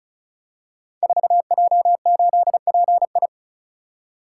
Morse code, heard as a keyed tone, about 36 wpm, 690 Hz. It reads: 4J8PI
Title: Code Groups - 5 Letter:Numbers - 35wpm